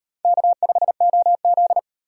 Morse code, heard as a keyed tone, about 38 wpm, 700 Hz. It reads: K5OZ